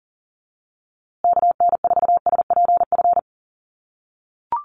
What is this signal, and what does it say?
Morse code, heard as a keyed tone, about 40 wpm, 700 Hz. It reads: KN4SPF
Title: USA Call Signs Repeated 3x 40wpm